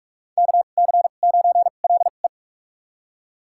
Morse code, heard as a keyed tone, about 41 wpm, spaced as for 31 wpm, 700 Hz. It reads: KC9LE
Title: USA Call Signs - 5 Characters - 2x Character Spacing - 45wpm